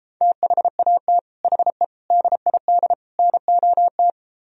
Morse code, heard as a keyed tone, about 33 wpm, 700 Hz. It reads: THAT HE DID NOT